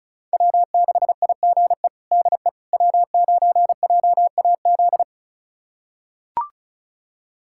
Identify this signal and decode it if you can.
Morse code, heard as a keyed tone, about 36 wpm, 700 Hz. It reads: W6IGE DE W9JAZ
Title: USA Callsign DE Callsign - 5 Characters - 35wpm